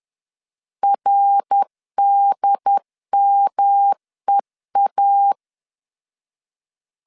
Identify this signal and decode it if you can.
Morse code, heard as a keyed tone, about 10 wpm, 800 Hz. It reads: RDMEA
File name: CW / DET / 1 3 1500C S B 10wpm 3335